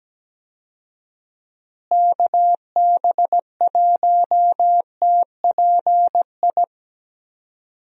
Morse code, heard as a keyed tone, about 17 wpm, 700 Hz. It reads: KB1TPI
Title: USA Call Signs - 6 Characters - Repeated 3x - 17wpm